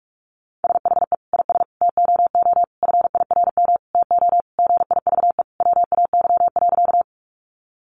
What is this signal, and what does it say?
Morse code, heard as a keyed tone, about 44 wpm, 700 Hz. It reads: SHE IS TOO FIRM TO GIVE WAY.
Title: Sentences from Top 1400 Words 45wpm